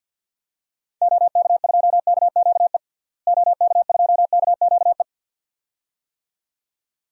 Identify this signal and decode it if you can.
Morse code, heard as a keyed tone, about 47 wpm, 700 Hz. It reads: OK2KQE OK2KQE
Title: DXCC-100 Call Signs - 6 Characters - Repeated 2x - 50wpm